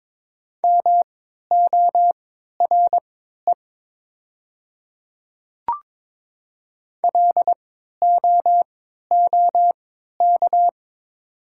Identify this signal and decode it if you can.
Morse code, heard as a keyed tone, about 22 wpm, spaced as for 13 wpm, 700 Hz. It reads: MORE LOOK
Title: 3x Character Spacing - Top 100 Words - 22wpm